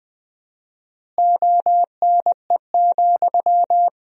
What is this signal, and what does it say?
Morse code, heard as a keyed tone, about 20 wpm, 700 Hz. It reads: ONE,
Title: 4x Word-Spacing - Sentences from Top 200 Words - 20wpm